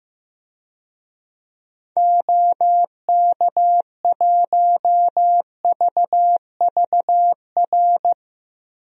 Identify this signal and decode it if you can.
Morse code, heard as a keyed tone, about 15 wpm, 700 Hz. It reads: OK1VVR
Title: DXCC-100 Call Signs - 6 Characters - Repeated 3x - 15wpm